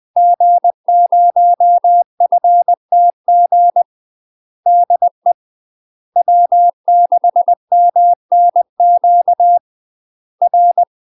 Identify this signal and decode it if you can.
Morse code, heard as a keyed tone, about 20 wpm, 700 Hz. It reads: G0FTG DE W6MNQ R
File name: UK MASTER QSOs TRAINING/20WPM